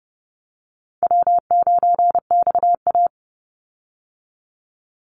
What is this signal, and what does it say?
Morse code, heard as a keyed tone, about 30 wpm, 700 Hz. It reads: W9XA
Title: Easy Call Signs Repeated 3x - 30wpm